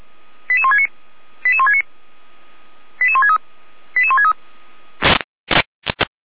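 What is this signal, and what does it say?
Five-tone selective call (ZVEI), heard as quick runs of five tones, tones of about 80 ms. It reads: 88169 88169 88163 88163
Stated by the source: ZVEI five-tone call 88167 received by SDR.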